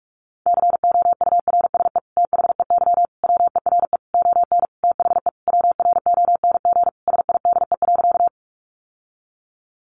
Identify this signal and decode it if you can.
Morse code, heard as a keyed tone, about 44 wpm, 700 Hz. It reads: COURSE THEY WERE ON THE WRONG SIDE.
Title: Sentences from Top 900 Words 45wpm